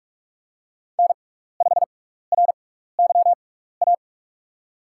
Morse code, heard as a keyed tone, about 44 wpm, spaced as for 15 wpm, 700 Hz. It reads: N5RYA
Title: USA Call Signs - 6x Character Spacing - 45wpm